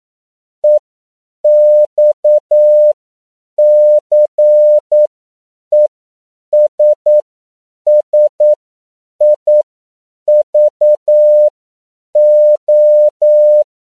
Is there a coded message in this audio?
Yes: Morse code from 0.6 to 13.6 s, heard as a keyed tone, about 9 wpm, spaced as for 7 wpm, 600 Hz. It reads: EXCESSIVO